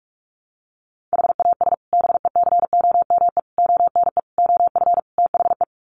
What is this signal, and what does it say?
Morse code, heard as a keyed tone, about 44 wpm, 700 Hz. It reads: HAS BECOME ONE OF THE
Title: Sets of 5 Words Encore 45wpm